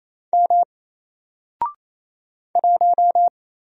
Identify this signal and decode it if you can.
Morse code, heard as a keyed tone, about 27 wpm, 700 Hz. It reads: M 1
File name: Single-Letter-Number-28wpm